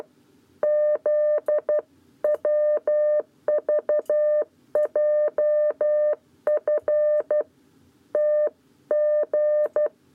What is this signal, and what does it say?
Morse code, heard as a keyed tone, about 12 wpm, 600 Hz. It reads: ZWVJF TG
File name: 7b lezione cw 1